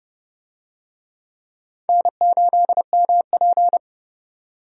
Morse code, heard as a keyed tone, about 30 wpm, 700 Hz. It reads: N8MP
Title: USA Call Signs - 4 Characters - 30wpm